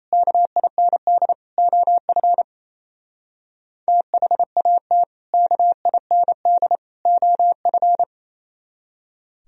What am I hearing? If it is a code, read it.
Morse code, heard as a keyed tone, about 34 wpm, 700 Hz. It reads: KIND OF THAT KIND OF
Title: Speed-Racing - Sets of 3 Words 28wpm